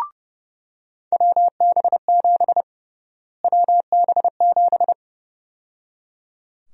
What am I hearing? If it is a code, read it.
Morse code, heard as a keyed tone, about 30 wpm, 700 Hz. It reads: WB7 WB7
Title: USA Call Sign Prefixes - Repeated 2x - 30wpm